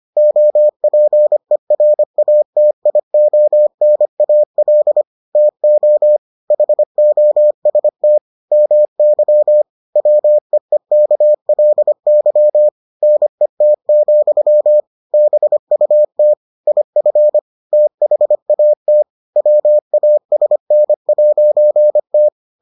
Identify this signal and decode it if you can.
Morse code, heard as a keyed tone, about 25 wpm, 600 Hz. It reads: OPERATIONAL TO HOST MY WEEKLY NET, BUT IF THAT WASN'T